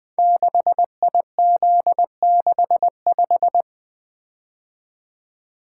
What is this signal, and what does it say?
Morse code, heard as a keyed tone, about 21 wpm, 700 Hz. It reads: 6IZ65
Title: Code Groups - 5 Letter:Numbers - 20wpm